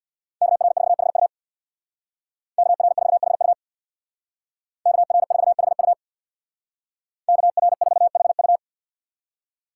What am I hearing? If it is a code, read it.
Morse code, heard as a keyed tone, about 64 wpm, 700 Hz. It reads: KR4HU KR4HU KR4HU KR4HU